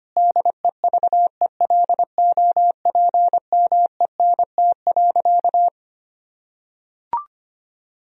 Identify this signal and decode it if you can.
Morse code, heard as a keyed tone, about 25 wpm, 700 Hz. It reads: DEVELOPMENT.
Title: Sentences from Top 1000 Words - Review Part 4 - 25wpm